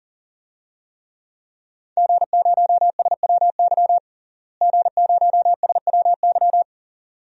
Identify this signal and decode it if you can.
Morse code, heard as a keyed tone, about 40 wpm, 700 Hz. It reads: G0SWY G0SWY
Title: DXCC-100 Call Signs - 5 Characters - Repeated 2x - 40wpm